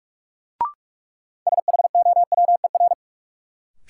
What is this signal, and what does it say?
Morse code, heard as a keyed tone, about 46 wpm, 700 Hz. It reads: SHOWER
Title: Sentences from Top 1600 Words - New Word Drill - 45wpm